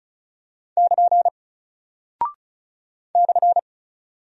Morse code, heard as a keyed tone, about 34 wpm, 700 Hz. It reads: <KN> /
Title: Single Character (Letter:Number:Punctuation:Prosign) Alt 35wpm